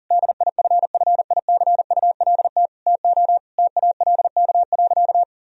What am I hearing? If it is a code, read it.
Morse code, heard as a keyed tone, about 40 wpm, 700 Hz. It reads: DIFFICULT TO TALK.
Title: Sentences from Top 1000 Words - Review Part 1 - 40wpm